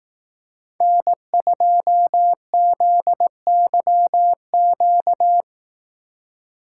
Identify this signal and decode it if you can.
Morse code, heard as a keyed tone, about 18 wpm, 700 Hz. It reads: N2ZYQ
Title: Speed Racing - USA 1x3 Call Signs 15wpm